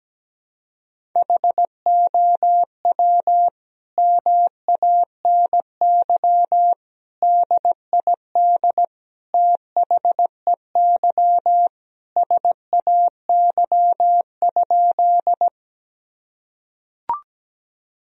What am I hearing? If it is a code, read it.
Morse code, heard as a keyed tone, about 17 wpm, 700 Hz. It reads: HOW MANY DID THEY SAY?
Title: Sentences from Top 1000 Words - Review Part 1 - 17wpm